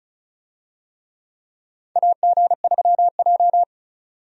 Morse code, heard as a keyed tone, about 34 wpm, 700 Hz. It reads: AG3J
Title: USA Call Signs - 4 Characters - 35wpm